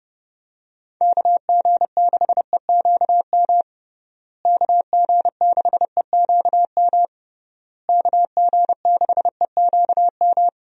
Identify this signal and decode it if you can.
Morse code, heard as a keyed tone, about 30 wpm, 700 Hz. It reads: KG6EQM KG6EQM KG6EQM
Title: USA Call Signs - 6 Characters - Repeated 3x - 30wpm